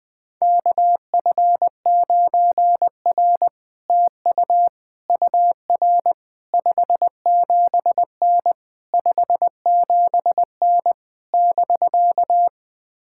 Morse code, heard as a keyed tone, about 20 wpm, 700 Hz. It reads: KF9R TU UR 57N 57N <BK>
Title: POTA QSO Practice - 20wpm